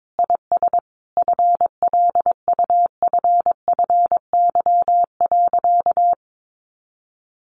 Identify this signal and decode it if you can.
Morse code, heard as a keyed tone, about 22 wpm, 700 Hz. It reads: IS FLUFFY.
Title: Easy Phrases 22wpm